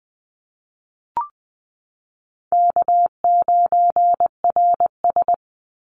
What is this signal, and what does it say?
Morse code, heard as a keyed tone, about 20 wpm, 700 Hz. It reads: K9RS
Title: Call Signs - 20wpm